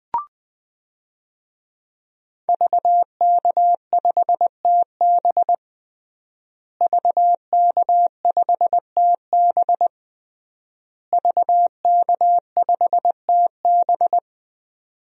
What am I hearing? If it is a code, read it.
Morse code, heard as a keyed tone, about 21 wpm, 700 Hz. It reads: VK5TB VK5TB VK5TB
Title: DXCC-100 Call Signs - 5 Characters - Repeated 3x - 20wpm